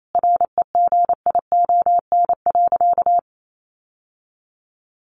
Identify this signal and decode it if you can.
Morse code, heard as a keyed tone, about 28 wpm, 700 Hz. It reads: REGION.
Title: Sentences from Top 1500 Words 28wpm